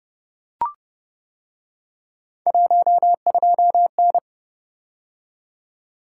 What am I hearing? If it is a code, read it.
Morse code, heard as a keyed tone, about 28 wpm, 700 Hz. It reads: 12N